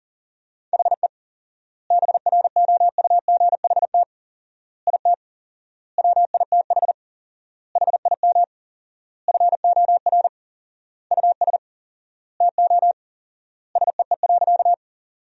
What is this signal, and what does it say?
Morse code, heard as a keyed tone, about 40 wpm, 700 Hz. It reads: HE BROUGHT IT WITH HIM FOR US TO SEE.
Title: 4x Word-Spacing - Sentences from Top 500 Words - 40wpm